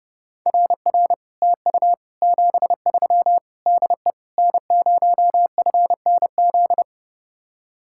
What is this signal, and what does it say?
Morse code, heard as a keyed tone, about 31 wpm, 700 Hz. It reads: RR TU 73 DE N0FNZ